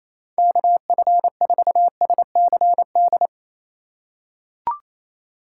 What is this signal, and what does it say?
Morse code, heard as a keyed tone, about 27 wpm, 700 Hz. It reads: KF4SCD
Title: USA Call Signs - 6 Characters - 4x Character Spacing - 28wpm